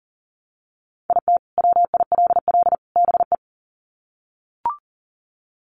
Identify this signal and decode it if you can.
Morse code, heard as a keyed tone, about 40 wpm, 700 Hz. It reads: IT WILL BE
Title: Sets of 3 Words 40wpm